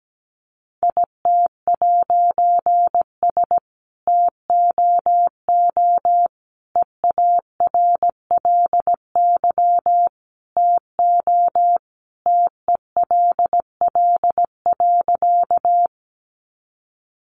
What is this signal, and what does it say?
Morse code, heard as a keyed tone, about 17 wpm, 700 Hz. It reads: IT'S TOO EARLY TO TELL.
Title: A Sound Of Thunder 17wpm